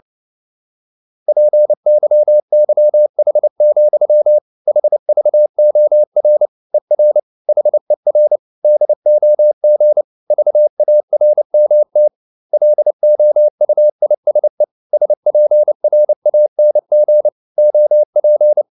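Morse code, heard as a keyed tone, about 29 wpm, 600 Hz. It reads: PYYH, HVOR ER HER DOG VARMT LOUISE SPRANG OP
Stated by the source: Fast (QRQ) Morse at 29 wpm, from CW med Gnister.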